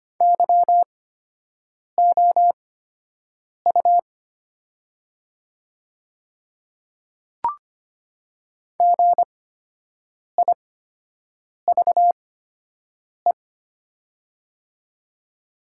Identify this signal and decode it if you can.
Morse code, heard as a keyed tone, about 25 wpm, spaced as for 7 wpm, 690 Hz. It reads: YOU GIVE
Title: 8x Character Spacing - Top 300 Words - 25wpm